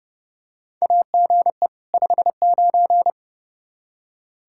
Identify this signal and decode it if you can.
Morse code, heard as a keyed tone, about 30 wpm, 700 Hz. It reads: AGE 59
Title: Practice QSO - Exchange 3 - More Info - 30wpm